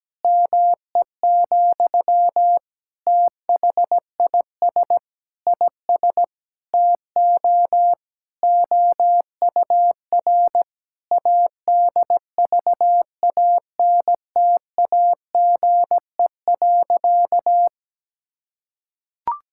Morse code, heard as a keyed tone, about 17 wpm, 700 Hz. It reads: ME, THIS IS TO OUR ADVANTAGE.
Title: Sentences from Top 1000 Words - Review Part 6 - 17wpm